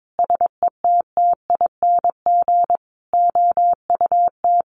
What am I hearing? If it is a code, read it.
Morse code, heard as a keyed tone, about 22 wpm, 700 Hz. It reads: SETTING OUT
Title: Sentences from Top 1300 Words 22wpm